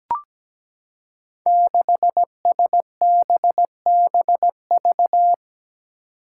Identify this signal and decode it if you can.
Morse code, heard as a keyed tone, about 17 wpm, 700 Hz. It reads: 6SBBV
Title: Five Challenging Letter-Number Combinations 17wpm